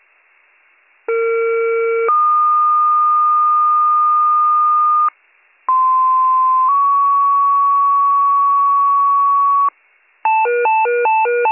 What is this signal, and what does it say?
Two-tone page, heard as a one-second tone, then a three-second tone, about 460 then 1220 Hz, about 1040 then 1120 Hz. It is heard from 1.1 to 9.7 s.